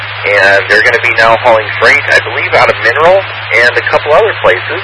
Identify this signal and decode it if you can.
Emergency Alert System (SAME) tones, heard as harsh buzzing bursts, with speech. It is heard from 0.4 to 3.8 s.